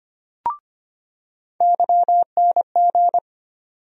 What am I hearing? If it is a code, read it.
Morse code, heard as a keyed tone, about 25 wpm, 700 Hz. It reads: YNG
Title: USA Call Sign Postfixes - 3 Letters 25wpm